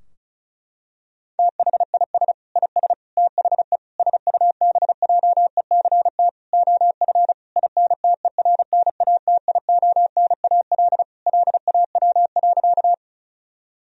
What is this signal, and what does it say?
Morse code, heard as a keyed tone, about 36 wpm, 700 Hz. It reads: THIS IS THE SUBJECT OF INTERNATIONAL LAW.
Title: Sentences from Top 1000 Words - Review Part 4 - 35wpm